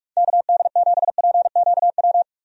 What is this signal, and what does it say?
Morse code, heard as a keyed tone, about 46 wpm, 700 Hz. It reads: KD7PQW
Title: USA Call Signs - 6 Characters - 45wpm